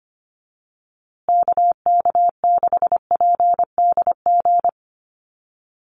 Morse code, heard as a keyed tone, about 25 wpm, 700 Hz. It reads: KK6PDG
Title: USA Call Signs Repeated 3x 25wpm